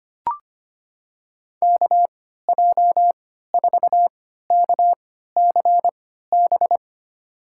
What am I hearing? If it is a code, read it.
Morse code, heard as a keyed tone, about 25 wpm, spaced as for 14 wpm, 700 Hz. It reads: KJ4KCB
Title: USA Call Signs - 6 Characters - 3x Character Spacing - 25wpm